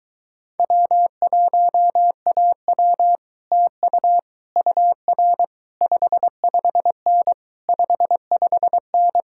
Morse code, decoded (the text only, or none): W1AW TU UR 55N 55N